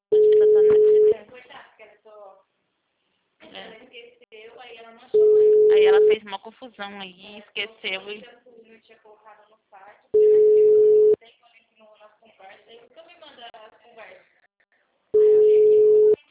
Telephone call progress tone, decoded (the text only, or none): ringback tone